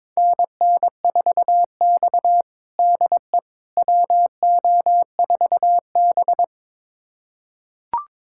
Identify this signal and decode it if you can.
Morse code, heard as a keyed tone, about 23 wpm, 700 Hz. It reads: NN4X DE WO4B
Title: USA Callsign DE Callsign - 4 Characters - 22wpm